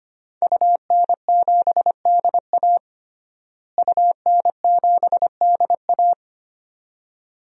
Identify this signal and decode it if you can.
Morse code, heard as a keyed tone, about 25 wpm, 700 Hz. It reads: UN7DA UN7DA
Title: DXCC-100 Call Signs - 5 Characters - Repeated 2x - 25wpm